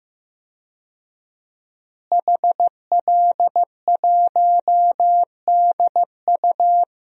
Morse code, heard as a keyed tone, about 15 wpm, 700 Hz. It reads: HL1DU
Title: DXCC-100 Call Signs - 5 Characters - Repeated 2x - 15wpm